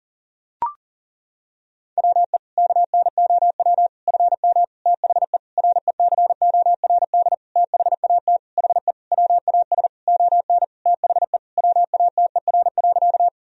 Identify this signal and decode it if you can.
Morse code, heard as a keyed tone, about 40 wpm, 700 Hz. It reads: WE KNOW FM THE RECORD THAT HE WAS ON THE WATER.
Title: Sentences from Top 500 Words and Abbreviations 40wpm